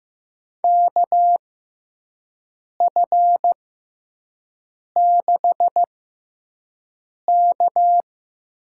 Morse code, heard as a keyed tone, about 15 wpm, spaced as for 5 wpm, 700 Hz. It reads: KF6K
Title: USA Call Signs - 6 Characters - 6x Character Spacing - 15wpm